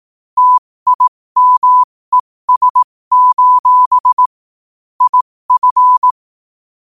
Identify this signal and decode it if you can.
Morse code, heard as a keyed tone, about 18 wpm, spaced as for 15 wpm, 1000 Hz. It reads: TIMES: IF